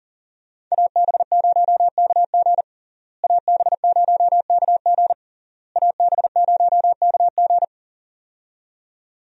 Morse code, heard as a keyed tone, about 37 wpm, 700 Hz. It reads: AB0KG AB0KG AB0KG
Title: USA Call Signs - 5 Characters - Repeated 3x - 40wpm